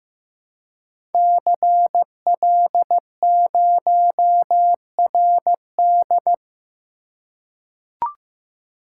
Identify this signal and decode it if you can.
Morse code, heard as a keyed tone, about 15 wpm, 700 Hz. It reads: CL0RD
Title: Code Groups - 5 Letter:Numbers - 15wpm